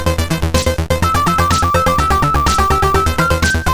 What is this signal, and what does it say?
There is no code.